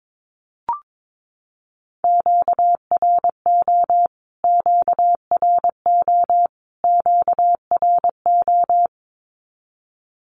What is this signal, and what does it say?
Morse code, heard as a keyed tone, about 22 wpm, 700 Hz. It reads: QRO QRO QRO
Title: Q-Codes Repeated 3x 22wpm